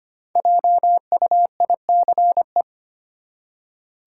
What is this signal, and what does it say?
Morse code, heard as a keyed tone, about 25 wpm, 690 Hz. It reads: JUICE
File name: practice / 25wpm / Easy Words with Hints - 25wpm